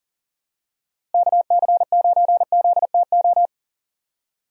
Morse code, heard as a keyed tone, about 40 wpm, 700 Hz. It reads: KC9ZTO
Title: USA Call Signs - 6 Characters - Repeated 3x - 40wpm